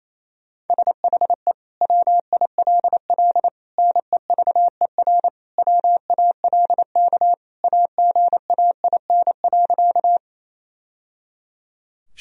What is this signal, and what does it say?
Morse code, heard as a keyed tone, about 27 wpm, 700 Hz. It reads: SHE WILL NEVER WALK AGAIN.
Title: Sentences from Top 1000 Words - Review Part 1 - 28wpm